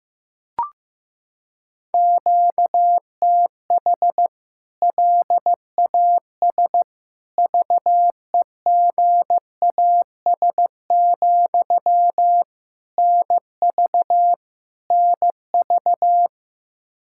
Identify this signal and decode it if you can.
Morse code, heard as a keyed tone, about 16 wpm, 700 Hz. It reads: QTH LAS VEGAS, NV NV
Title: K1USN SST QSO Practice 15wpm